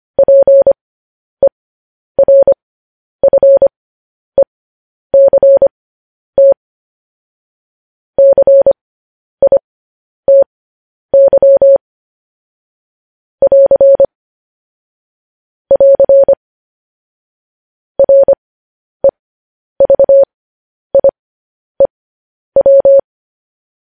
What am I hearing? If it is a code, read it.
Morse code, heard as a keyed tone, about 25 wpm, spaced as for 10 wpm, 550 Hz. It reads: PERFECT CITY <AR> <AR> REVIEW